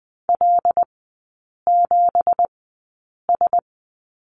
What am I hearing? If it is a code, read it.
Morse code, heard as a keyed tone, about 20 wpm, spaced as for 8 wpm, 690 Hz. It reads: L7S
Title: 166 Lesson 27e - Number 7 - Three Characters 20wpm